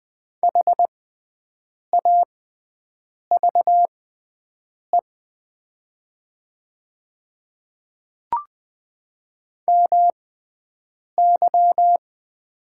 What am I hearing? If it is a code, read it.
Morse code, heard as a keyed tone, about 20 wpm, spaced as for 7 wpm, 700 Hz. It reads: HAVE MY
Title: 6x Character Spacing - Top 100 Words - 20wpm